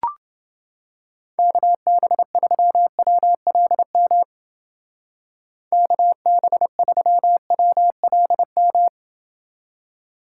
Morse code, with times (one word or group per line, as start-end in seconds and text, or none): 1.4-4.2 KB3WLM
5.7-8.9 KB3WLM